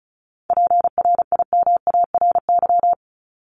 Morse code, heard as a keyed tone, about 34 wpm, 700 Hz. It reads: PRIMARY
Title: LY Suffix - Top 100 Words with Base Words 35wpm